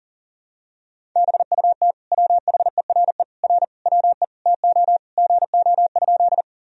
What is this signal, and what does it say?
Morse code, heard as a keyed tone, about 37 wpm, 700 Hz. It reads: BUT WHERE R WE TO GO?
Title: Sentences from Top 200 Words and Abbreviations 40wpm